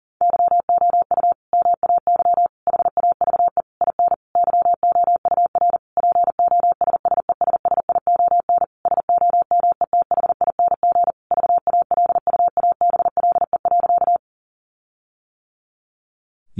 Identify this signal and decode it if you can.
Morse code, heard as a keyed tone, about 40 wpm, 700 Hz. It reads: YOU MAY HAVE IN YOUR POSSESSION SOMETHING VALUABLE.
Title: Sentences from Top 1400 Words 40wpm